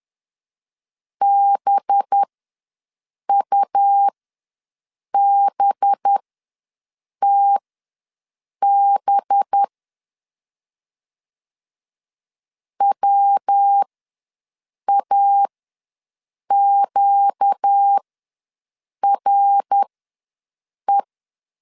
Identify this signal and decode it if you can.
Morse code, heard as a keyed tone, about 10 wpm, spaced as for 6 wpm, 800 Hz. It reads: BUBTB WAQRE
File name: LEZIONI / 5 LBVC / 5 LB ws 10wpm 3616